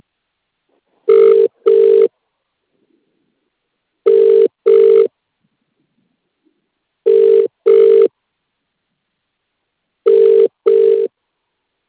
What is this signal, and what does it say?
Ringback tone (UK), 400 + 450 Hz.